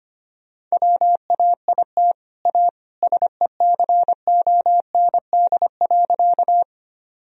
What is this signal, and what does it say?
Morse code, heard as a keyed tone, about 25 wpm, 700 Hz. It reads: WAIT A SECOND.
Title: Sentences from Top 1000 Words - Review Part 1 - 25wpm